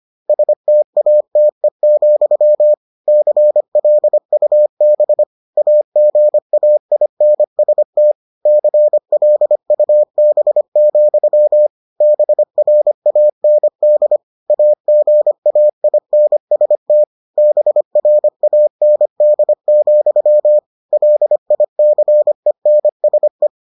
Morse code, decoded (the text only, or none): STATE, CLUB AGAINST CLUB, BRAND AGAINST BRAND, LICENSE